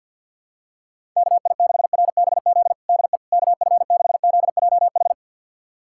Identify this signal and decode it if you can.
Morse code, heard as a keyed tone, about 49 wpm, 700 Hz. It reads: KI6RBZ DE KF6ZJH
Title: USA Callsign DE Callsign - 6 Characters - 50wpm